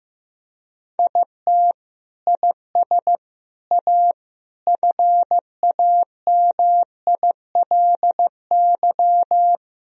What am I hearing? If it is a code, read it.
Morse code, heard as a keyed tone, about 15 wpm, 700 Hz. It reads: IT IS A FAMILY
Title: Sentences from Top 1000 Words - Review Part 2 - 15wpm